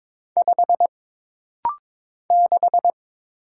Morse code, heard as a keyed tone, about 22 wpm, 700 Hz. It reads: 5 6
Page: This rapid-fire timing is geared to push your ICR (Instant Character Recognition) skills.